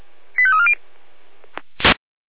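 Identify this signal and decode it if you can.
Five-tone selective call (ZVEI), heard as quick runs of five tones, tones of about 70 ms. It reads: 85340
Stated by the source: ZVEI five-tone selective call 85349 received by SDR.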